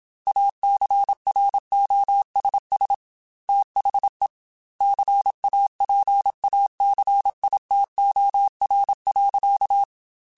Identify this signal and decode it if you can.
Morse code, heard as a keyed tone, about 27 wpm, 800 Hz. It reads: ACROSS THE CAPACITOR.
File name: Technical Subjects / Technical Subjects-mp3s / Reactance @25